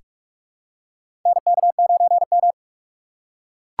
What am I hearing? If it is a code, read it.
Morse code, heard as a keyed tone, about 43 wpm, 700 Hz. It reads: NK9M
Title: USA Call Signs - 4x Character Spacing - 45wpm